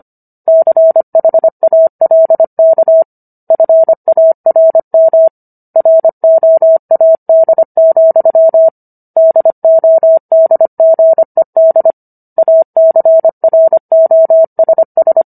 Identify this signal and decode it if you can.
Morse code, heard as a keyed tone, about 25 wpm, 650 Hz. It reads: CHALK FARM ROAD, DODGED ACROSS